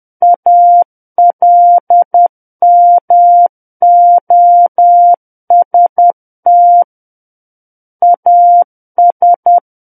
Morse code, heard as a keyed tone, about 10 wpm, 700 Hz. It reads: ALMOST AS
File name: Book One / 01-Chapter 1 - 10 WPM